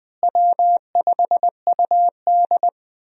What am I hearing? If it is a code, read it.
Morse code, heard as a keyed tone, about 20 wpm, 700 Hz. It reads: W5UD
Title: USA Call Signs - 4 Characters - Repeated 3x - 20wpm